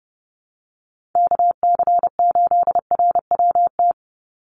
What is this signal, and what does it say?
Morse code, heard as a keyed tone, about 30 wpm, 700 Hz. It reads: KC8RWT
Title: USA Call Signs 30wpm